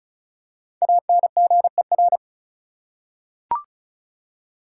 Morse code, heard as a keyed tone, about 36 wpm, 700 Hz. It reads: ANGER
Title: Five Letter Words - 35wpm